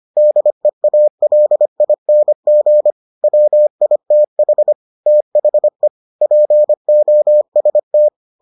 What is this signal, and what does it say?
Morse code, heard as a keyed tone, about 25 wpm, 600 Hz. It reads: DEALING WITH THE POST